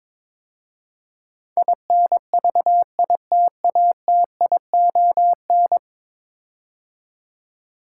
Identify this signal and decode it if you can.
Morse code, heard as a keyed tone, about 22 wpm, 700 Hz. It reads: INVITATION
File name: practice / 22wpm / 10 Letter Words - First 3 Letters Given - 22wpm